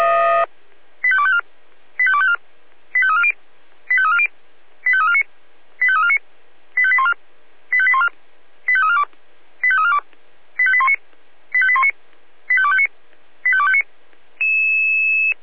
Five-tone selective call (ZVEI), heard as quick runs of five tones, tones of about 70 ms. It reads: 85254 85254 85340 85340 85349 85349 85814 85814 85342 85342 85819 85819 85259 85259
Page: SDR/ZVEI